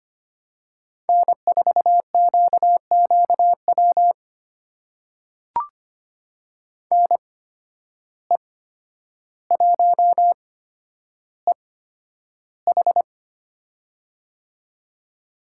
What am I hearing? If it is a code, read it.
Morse code, heard as a keyed tone, about 25 wpm, spaced as for 11 wpm, 700 Hz. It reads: N4QQW NE1EH
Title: USA Call Signs - 5 Characters - 8x Character Spacing - 25wpm